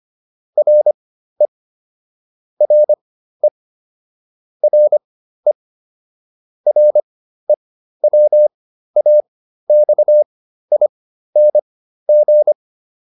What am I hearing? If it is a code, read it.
Morse code, heard as a keyed tone, about 25 wpm, spaced as for 13 wpm, 600 Hz. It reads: RE RE RE REWAXING
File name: RE8 13